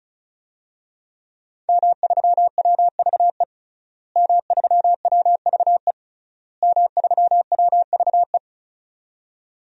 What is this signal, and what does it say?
Morse code, heard as a keyed tone, about 36 wpm, 700 Hz. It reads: M3WVE M3WVE M3WVE
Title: DXCC-100 Call Signs - 5 Characters - Repeated 3x - 35wpm